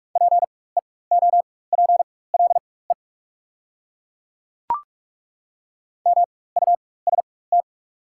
Morse code, heard as a keyed tone, about 43 wpm, spaced as for 21 wpm, 700 Hz. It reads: PEOPLE MUST
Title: 4x Character Spacing - Top 300 Words - 45wpm